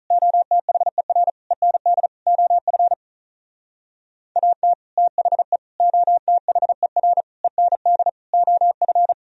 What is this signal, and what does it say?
Morse code, heard as a keyed tone, about 44 wpm, 700 Hz. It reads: OTHER END OF AT THE OTHER END OF